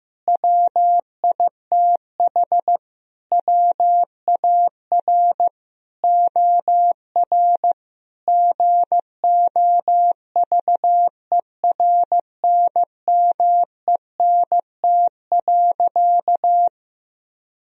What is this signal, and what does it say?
Morse code, heard as a keyed tone, about 15 wpm, 700 Hz. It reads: WITH WAR OR GOVERNMENT.
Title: Sentences from Top 1000 Words - Review Part 3 - 15wpm